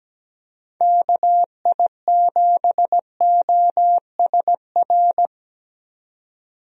Morse code, heard as a keyed tone, about 17 wpm, 700 Hz. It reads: KI7OSR
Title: USA Call Signs - 6 Characters - 17wpm